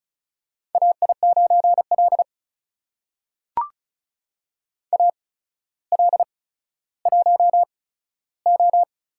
Morse code, heard as a keyed tone, about 36 wpm, spaced as for 15 wpm, 700 Hz. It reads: AI9L AL1O
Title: USA Call Signs - 4 Characters - 8x Character Spacing - 35wpm